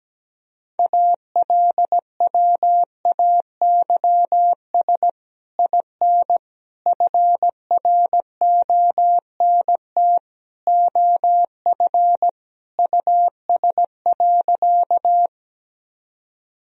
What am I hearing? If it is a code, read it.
Morse code, heard as a keyed tone, about 17 wpm, 700 Hz. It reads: ALWAYS IN FRONT OF US.
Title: Sentences from Top 1000 Words - Review Part 2 - 17wpm